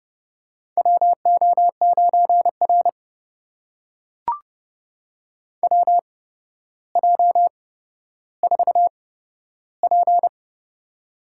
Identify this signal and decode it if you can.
Morse code, heard as a keyed tone, about 30 wpm, spaced as for 13 wpm, 700 Hz. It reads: WO9R WJ4P
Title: USA Call Signs - 4 Characters - 8x Character Spacing - 30wpm